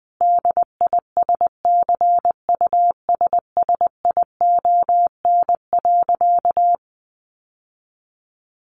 Morse code, heard as a keyed tone, about 20 wpm, 700 Hz. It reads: DISCUSSION.
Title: Sentences from Top 1100 Words 20wpm